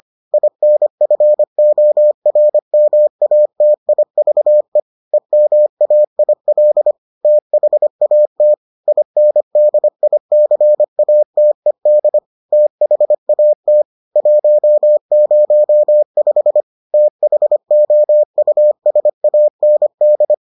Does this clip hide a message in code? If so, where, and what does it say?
Yes: Morse code from 0.3 to 20.3 s, heard as a keyed tone, about 25 wpm, 600 Hz. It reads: INFORMATIVE EMAIL THAT INDICATED THAT 105 THOUSAND